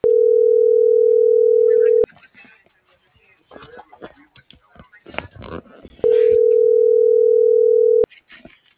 Ringback tone (North America), 440 + 480 Hz.